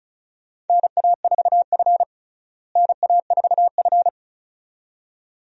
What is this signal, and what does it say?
Morse code, heard as a keyed tone, about 36 wpm, 700 Hz. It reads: NA4F NA4F
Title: USA Call Signs - 4 Characters - Repeated 2x - 35wpm